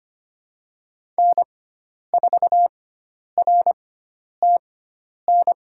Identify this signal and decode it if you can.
Morse code, heard as a keyed tone, about 25 wpm, spaced as for 10 wpm, 700 Hz. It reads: N4RTN
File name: Lesson 16g - Letter c - Call Signs 25wpm